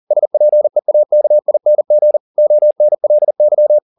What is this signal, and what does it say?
Morse code, heard as a keyed tone, about 40 wpm, 600 Hz. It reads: SPEAKING ONLY
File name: the old man and the sea 2 40wpm